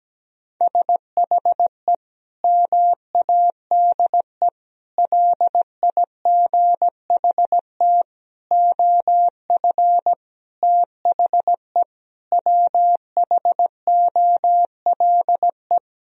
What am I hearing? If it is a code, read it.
Morse code, heard as a keyed tone, about 17 wpm, 700 Hz. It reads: SHE MADE LIGHT OF THE WHOLE
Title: Sentences from Top 1000 Words - Review Part 1 - 17wpm